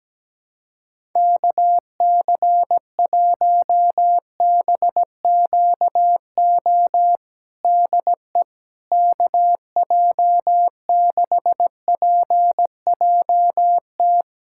Morse code, heard as a keyed tone, about 17 wpm, 700 Hz. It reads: KC1BQO DE KJ6PJT
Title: USA Callsign DE Callsign - 6 Characters - 17wpm